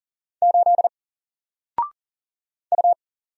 Morse code, heard as a keyed tone, about 40 wpm, 700 Hz. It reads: 8 U
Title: Single Letter-Number 40wpm